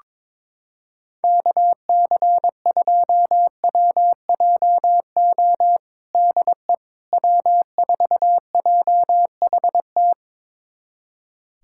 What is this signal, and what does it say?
Morse code, heard as a keyed tone, about 22 wpm, 700 Hz. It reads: KC2WJO DE W4JHT
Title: USA Callsign DE Callsign 22wpm